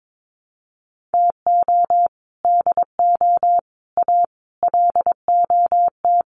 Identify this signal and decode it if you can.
Morse code, heard as a keyed tone, about 22 wpm, 700 Hz. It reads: TO DO A LOT
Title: Sets of 5 Words Encore 22wpm